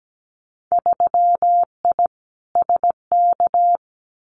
Morse code, heard as a keyed tone, about 17 wpm, 700 Hz. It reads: 3I SK
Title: Winter Field Day Exchanges Encore 17wpm